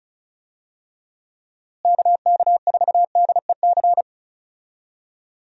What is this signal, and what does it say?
Morse code, heard as a keyed tone, about 36 wpm, 700 Hz. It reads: KK4DEC
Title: USA Call Signs - 6 Characters - Repeated 2x - 35wpm